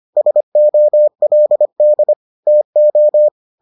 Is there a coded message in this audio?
Yes: Morse code from 0.2 to 3.3 s, heard as a keyed tone, about 25 wpm, 600 Hz. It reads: SOLD TO